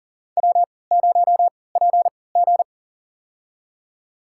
Morse code, heard as a keyed tone, about 38 wpm, spaced as for 22 wpm, 700 Hz. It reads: W0PG
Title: USA Call Signs - 4 Characters - 3x Character Spacing - 40wpm